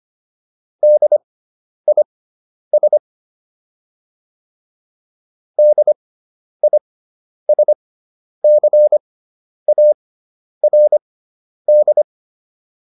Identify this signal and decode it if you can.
Morse code, heard as a keyed tone, about 25 wpm, spaced as for 10 wpm, 600 Hz. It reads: DIS DISCARD